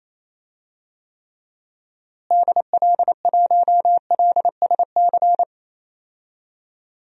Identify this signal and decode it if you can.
Morse code, heard as a keyed tone, about 28 wpm, 700 Hz. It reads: DL1LSC
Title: DXCC-100 Call Signs - 6 Characters - Repeated 2x - 28wpm